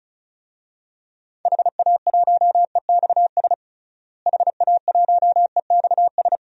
Morse code, heard as a keyed tone, about 36 wpm, 700 Hz. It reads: HA1EXS HA1EXS
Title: DXCC-100 Call Signs - 6 Characters - Repeated 3x - 35wpm